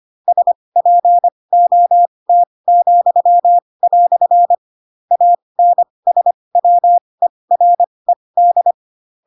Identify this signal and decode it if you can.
Morse code, heard as a keyed tone, about 25 wpm, spaced as for 20 wpm, 700 Hz. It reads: SPOT," ANSWERED
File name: Pinocchio / Pinocchio ch4 15 25 ik1hge